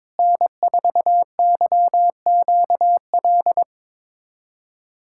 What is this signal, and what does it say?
Morse code, heard as a keyed tone, about 22 wpm, 700 Hz. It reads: N4YQL
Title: USA Callsign DE Callsign - 5 Characters - 22wpm